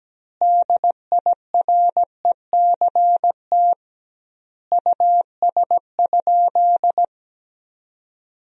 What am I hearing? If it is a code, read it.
Morse code, heard as a keyed tone, about 17 wpm, 700 Hz. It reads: DIRECT US?
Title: 2x Word-Spacing - Sentences from Top 400 Words - 17wpm